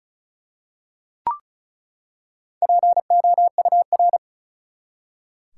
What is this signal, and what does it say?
Morse code, heard as a keyed tone, about 34 wpm, 700 Hz. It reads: POUR
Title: Four Letter Words - 35wpm